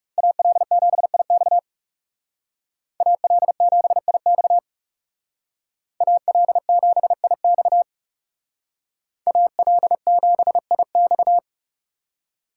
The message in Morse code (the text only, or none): AL7IX AL7IX AL7IX AL7IX